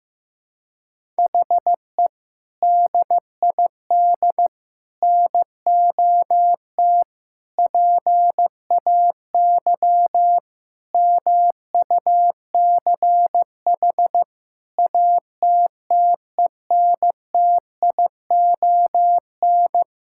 Morse code, heard as a keyed tone, about 15 wpm, 700 Hz. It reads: HE DID NOT PAY MUCH ATTENTION
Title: Sentences from Top 1000 Words - Review Part 6 - 15wpm